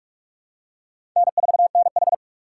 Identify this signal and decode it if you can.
Morse code, heard as a keyed tone, about 46 wpm, 690 Hz. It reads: N4NH
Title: USA Call Signs - 4 Characters - 3x Character Spacing - 45wpm